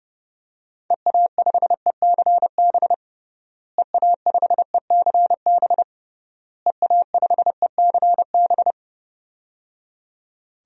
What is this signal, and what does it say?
Morse code, heard as a keyed tone, about 30 wpm, 700 Hz. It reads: EA5ECB EA5ECB EA5ECB
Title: DXCC-100 Call Signs - 6 Characters - Repeated 3x - 30wpm